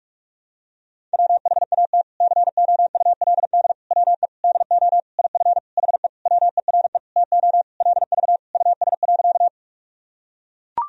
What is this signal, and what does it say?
Morse code, heard as a keyed tone, about 42 wpm, 700 Hz. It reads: WHAT COULD WE DO IF HE WERE TO LV US.
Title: Sentences from Top 400 Words and Abbreviations 45wpm